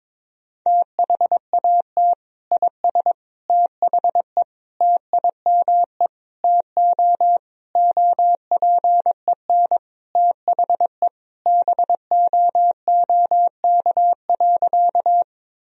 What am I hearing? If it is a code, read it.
Morse code, heard as a keyed tone, about 22 wpm, 700 Hz. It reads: THAT IS THE TIME TO OPEN THE BOOK.
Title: Sentences from Top 1000 Words - Review Part 1 - 22wpm